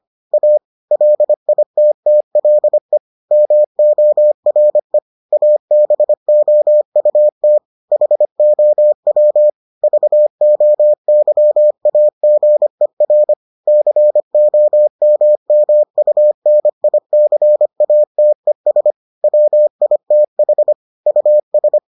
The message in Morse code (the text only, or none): A LITTLE MORE ABOUT HOW VOYAGER COMMUNICATES WITH US